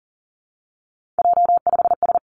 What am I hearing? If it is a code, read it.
Morse code, heard as a keyed tone, about 39 wpm, 700 Hz. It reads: J5S
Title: Three Challenging Letter-Number Combinations 40wpm